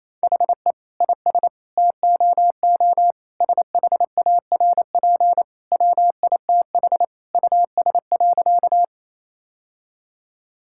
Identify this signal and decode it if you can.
Morse code, heard as a keyed tone, about 28 wpm, 700 Hz. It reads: HE IS TOO SHARP WITH US.
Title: Sentences from Top 1600 Words 28wpm